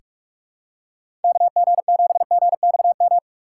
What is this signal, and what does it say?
Morse code, heard as a keyed tone, about 48 wpm, 700 Hz. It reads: KG7GXM
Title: USA Call Signs - 6 Characters - 8x Character Spacing - 45wpm